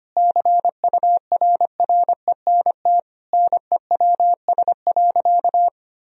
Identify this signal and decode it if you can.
Morse code, heard as a keyed tone, about 25 wpm, 700 Hz. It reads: CURRENT NEWS.
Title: Sentences from Top 1000 Words - Review Part 4 - 25wpm